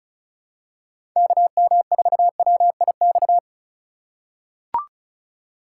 Morse code, heard as a keyed tone, about 34 wpm, 700 Hz. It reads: KM4WIX
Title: USA Call Signs - 2x Character Spacing - 35wpm